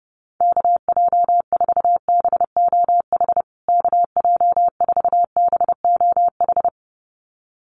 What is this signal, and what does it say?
Morse code, heard as a keyed tone, about 30 wpm, 700 Hz. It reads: KJ4BOH KJ4BOH
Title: USA Call Signs Repeated 2x 30wpm